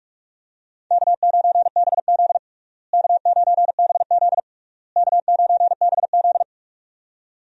Morse code, heard as a keyed tone, about 46 wpm, 700 Hz. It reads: K9BZ K9BZ K9BZ
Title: USA Call Signs - 4 Characters - Repeated 3x - 45wpm